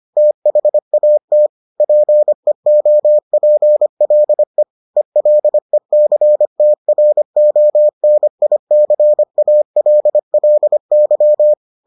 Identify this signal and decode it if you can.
Morse code, heard as a keyed tone, about 25 wpm, 600 Hz. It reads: THAT PEOPLE ELECTRONICALLY